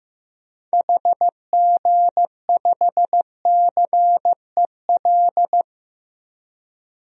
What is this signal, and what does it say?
Morse code, heard as a keyed tone, about 15 wpm, 700 Hz. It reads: HG5CEL
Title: DXCC-100 Call Signs - 6 Characters - 15wpm